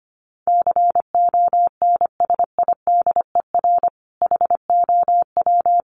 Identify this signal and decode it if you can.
Morse code, heard as a keyed tone, about 25 wpm, 700 Hz. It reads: CONSIDER HOW
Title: Sentences from Top 1000 Words 25wpm